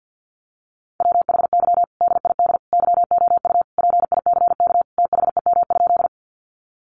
Morse code, heard as a keyed tone, about 50 wpm, 700 Hz. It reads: WHY DID YOU PICK THEM?
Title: Sentences from Top 1000 Words 50wpm